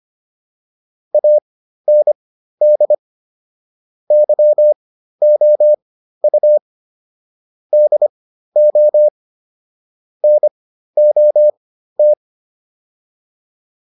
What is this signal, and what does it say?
Morse code, heard as a keyed tone, about 25 wpm, spaced as for 13 wpm, 600 Hz. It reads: AND YOU DO NOT